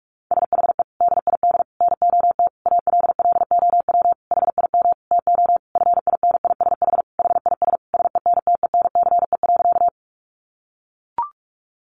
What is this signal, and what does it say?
Morse code, heard as a keyed tone, about 44 wpm, 700 Hz. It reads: SHE DID NOT ALLOW HIM TO FINISH HIS SENTENCE.
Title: Sentences from Top 1200 Words 45wpm